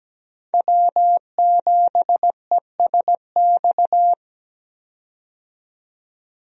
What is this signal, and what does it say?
Morse code, heard as a keyed tone, about 17 wpm, 700 Hz. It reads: W7ESX
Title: USA Call Signs - 5 Characters - 17wpm